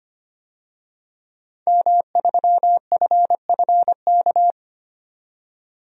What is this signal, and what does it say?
Morse code, heard as a keyed tone, about 25 wpm, 700 Hz. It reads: M3FFK
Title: DXCC-100 Call Signs - 5 Characters - 25wpm